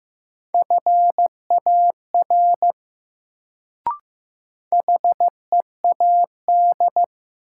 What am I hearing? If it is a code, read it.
Morse code, heard as a keyed tone, about 15 wpm, 700 Hz. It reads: FAR HEAD
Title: Sentences from Top 300 Words - New Word Drill - 15wpm